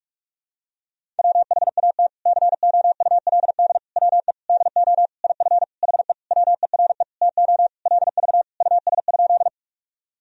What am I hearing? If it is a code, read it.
Morse code, heard as a keyed tone, about 46 wpm, 700 Hz. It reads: WHAT COULD WE DO IF HE WERE TO LV US?
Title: Sentences from Top 400 Words and Abbreviations 45wpm